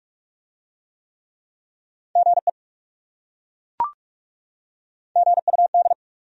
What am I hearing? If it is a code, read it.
Morse code, heard as a keyed tone, about 40 wpm, 700 Hz. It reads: GE GUD